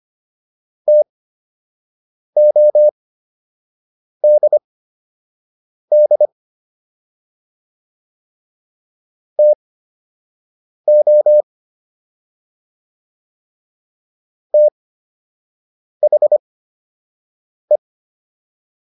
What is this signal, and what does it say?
Morse code, heard as a keyed tone, about 25 wpm, spaced as for 6 wpm, 600 Hz. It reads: TODD TO THE